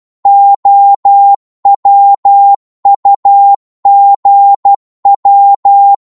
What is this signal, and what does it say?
Morse code, heard as a keyed tone, about 12 wpm, 800 Hz. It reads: OWUGW
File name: Morse / ltr-12wpm